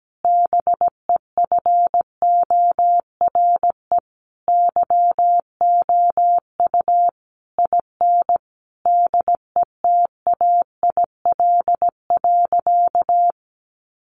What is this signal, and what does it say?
Morse code, heard as a keyed tone, about 17 wpm, 700 Hz. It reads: BEFORE YOU IN DETAIL.
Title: Sentences from Top 1500 Words 17wpm